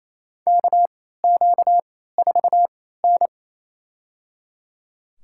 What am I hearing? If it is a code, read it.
Morse code, heard as a keyed tone, about 28 wpm, spaced as for 16 wpm, 700 Hz. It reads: KQ4N
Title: USA Call Signs - 4 Characters - 3x Character Spacing - 28wpm